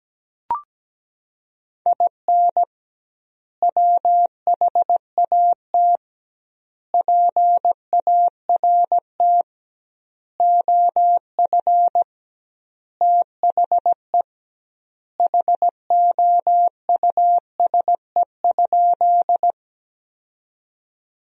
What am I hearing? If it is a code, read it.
Morse code, heard as a keyed tone, about 17 wpm, 700 Hz. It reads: IN WHAT PART OF THE HOUSE?
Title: 2x Word-Spacing - Sentences from Top 200 Words - 17wpm